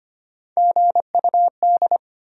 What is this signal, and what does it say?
Morse code, heard as a keyed tone, about 34 wpm, 700 Hz. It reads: GUD